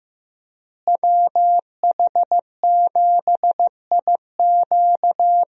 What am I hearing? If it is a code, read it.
Morse code, heard as a keyed tone, about 16 wpm, 700 Hz. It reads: WH7IQ
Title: USA Call Signs - 5 Characters - Repeated 3x - 15wpm